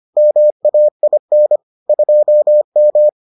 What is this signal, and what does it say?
Morse code, heard as a keyed tone, about 25 wpm, 600 Hz. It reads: MAIN 2M